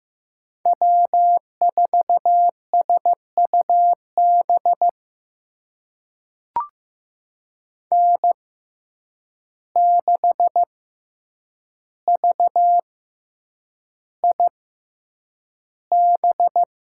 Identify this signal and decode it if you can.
Morse code, heard as a keyed tone, about 15 wpm, spaced as for 8 wpm, 700 Hz. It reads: W4SUB N6VIB
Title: USA Call Signs - 6x Character Spacing - 15wpm